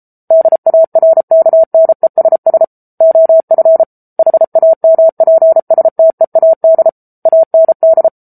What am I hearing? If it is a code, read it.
Morse code, heard as a keyed tone, about 34 wpm, 650 Hz. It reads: DARKNESS OF HAMPSTEAD AND
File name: Book Two / 07-Chapter 7 - 33 WPM